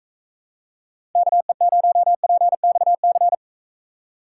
Morse code, heard as a keyed tone, about 42 wpm, 700 Hz. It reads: KE0PXC